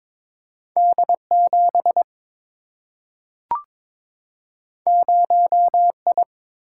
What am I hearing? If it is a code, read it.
Morse code, heard as a keyed tone, about 22 wpm, 700 Hz. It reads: D 7 0 I
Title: Single-Letter Single-Number 22wpm